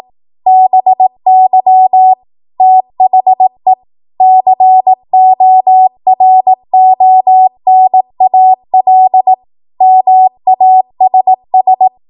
Morse code, heard as a keyed tone, about 18 wpm, 750 Hz. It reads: BY THE CORONAL MASS
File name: Morse / 18WPM